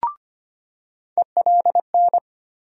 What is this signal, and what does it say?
Morse code, heard as a keyed tone, about 25 wpm, 700 Hz. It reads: ELN
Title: USA Call Sign Postfixes - 3 Letters 25wpm